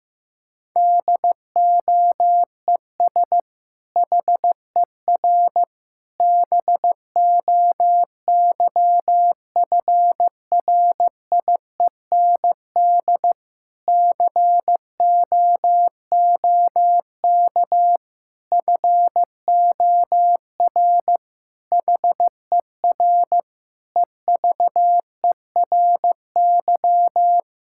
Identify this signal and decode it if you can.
Morse code, heard as a keyed tone, about 15 wpm, 700 Hz. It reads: DOES HER BOYFRIEND COOK FOR HER EVERY
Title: Sentences from Top 1000 Words - Review Part 5 - 15wpm